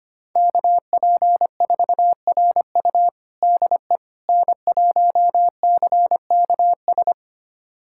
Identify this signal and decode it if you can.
Morse code, heard as a keyed tone, about 25 wpm, 700 Hz. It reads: KP4RU DE N1CKS
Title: USA Callsign DE Callsign - 5 Characters - 25wpm